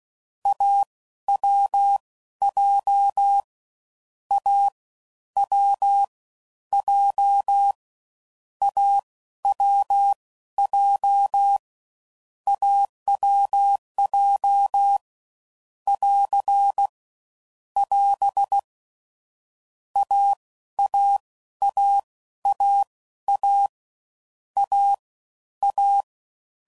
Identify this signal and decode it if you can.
Morse code, heard as a keyed tone, about 16 wpm, spaced as for 12 wpm, 800 Hz. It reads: AWJ A W J AWJ AWJ <AR> <AS> AAAAA A A
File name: Combinations / Difficult Combos / AWJ1